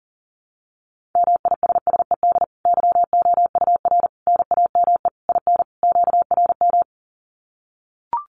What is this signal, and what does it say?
Morse code, heard as a keyed tone, about 40 wpm, 700 Hz. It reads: MISSED YOUR NAME IN QRM
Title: Common QSO Phrases 40wpm